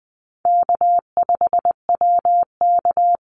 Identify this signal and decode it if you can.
Morse code, heard as a keyed tone, about 20 wpm, 700 Hz. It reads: K5WK
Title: Easy Call Signs - 20wpm